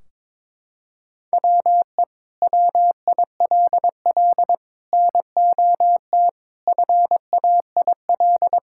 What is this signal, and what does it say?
Morse code, heard as a keyed tone, about 22 wpm, 700 Hz. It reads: WE WILL NOT FAIL
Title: Sentences from Top 1600 Words 22wpm